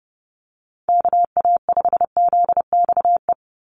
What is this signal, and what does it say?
Morse code, heard as a keyed tone, about 28 wpm, 700 Hz. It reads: KA5ZXE
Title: USA Call Signs 30wpm